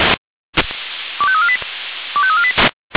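Five-tone selective call (ZVEI), heard as quick runs of five tones, tones of about 70 ms. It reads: 25348 25348
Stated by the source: ZVEI five-tone selective call received on SDR.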